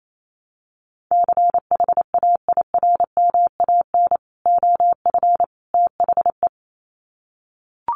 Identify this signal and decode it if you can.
Morse code, heard as a keyed tone, about 28 wpm, 700 Hz. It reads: CHAIRMAN OF THE
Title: Sets of 3 Words 28wpm